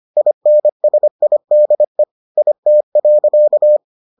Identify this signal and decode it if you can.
Morse code, heard as a keyed tone, about 25 wpm, 600 Hz. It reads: INSIDE IT.